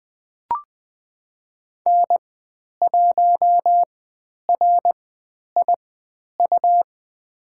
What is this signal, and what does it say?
Morse code, heard as a keyed tone, about 20 wpm, spaced as for 10 wpm, 700 Hz. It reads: N1RIU
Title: 109 Lesson 18g - Letter m - Call Signs 20wpm